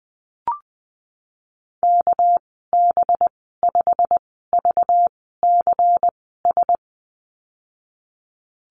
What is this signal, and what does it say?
Morse code, heard as a keyed tone, about 20 wpm, spaced as for 15 wpm, 700 Hz. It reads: KB5VCS
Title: 231 Lesson 37g - Letter x - Call Signs 20wpm